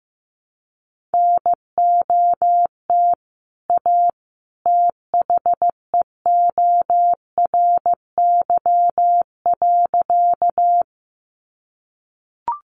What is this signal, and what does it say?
Morse code, heard as a keyed tone, about 14 wpm, 700 Hz. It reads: NOT A THEORY.
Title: Sentences from Top 600 Words 15wpm